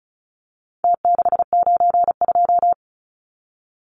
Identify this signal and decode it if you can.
Morse code, heard as a keyed tone, about 34 wpm, 700 Hz. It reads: T692
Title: Numbers - Contest Style - 4-Digit 35wpm